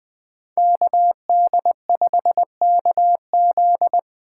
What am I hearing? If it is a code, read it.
Morse code, heard as a keyed tone, about 20 wpm, 700 Hz. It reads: KD5KZ
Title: USA Call Signs - 5 Characters - 4x Character Spacing - 20wpm